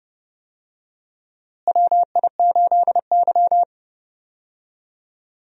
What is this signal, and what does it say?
Morse code, heard as a keyed tone, about 30 wpm, 700 Hz. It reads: WI8Y
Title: USA Call Signs - 4 Characters - Repeated 3x - 30wpm